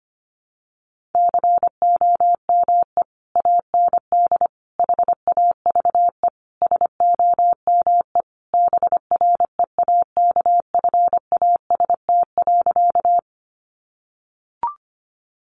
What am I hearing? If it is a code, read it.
Morse code, heard as a keyed tone, about 25 wpm, 700 Hz. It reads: COME AND HAVE SOME BREAKFAST.
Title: Sentences from Top 1500 Words 25wpm